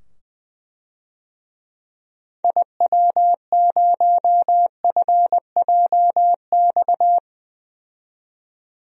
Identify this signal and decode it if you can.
Morse code, heard as a keyed tone, about 21 wpm, 700 Hz. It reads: IW0FJX
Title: DXCC-100 Call Signs - 6 Characters - 20wpm